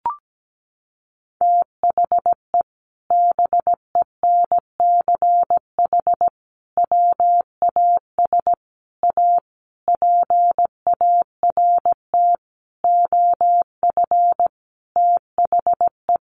Morse code, heard as a keyed tone, about 17 wpm, 700 Hz. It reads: THE BENCH WAS A PART OF THE
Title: Sentences from Top 1400 Words 17wpm